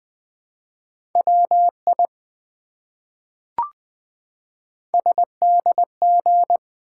Morse code, heard as a keyed tone, about 20 wpm, 700 Hz. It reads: WI SDG
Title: ARRL Sections v2 - 20wpm